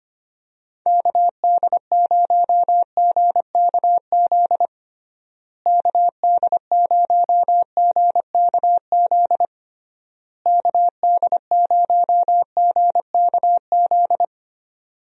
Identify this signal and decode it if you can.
Morse code, heard as a keyed tone, about 25 wpm, 700 Hz. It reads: KD0GKZ KD0GKZ KD0GKZ
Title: USA Call Signs - 6 Characters - Repeated 3x - 25wpm